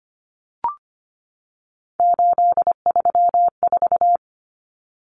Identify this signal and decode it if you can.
Morse code, heard as a keyed tone, about 25 wpm, 700 Hz. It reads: 834
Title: Common Cut Numbers - 3-Digit 25wpm